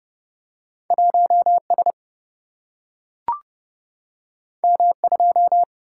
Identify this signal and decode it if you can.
Morse code, heard as a keyed tone, about 30 wpm, 700 Hz. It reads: 1 S M 2
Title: Single-Letter Single-Number 30wpm